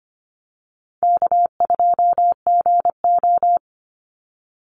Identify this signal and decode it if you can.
Morse code, heard as a keyed tone, about 25 wpm, 700 Hz. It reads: K2GO
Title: Easy Call Signs - 25wpm